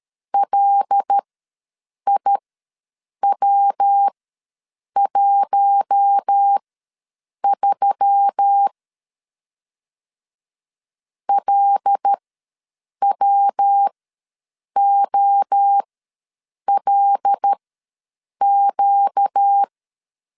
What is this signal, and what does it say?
Morse code, heard as a keyed tone, about 13 wpm, spaced as for 7 wpm, 800 Hz. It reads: LIW13 LWOLQ
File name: LEZIONI / 5 LBVC / 5 L ws 12wpm 2958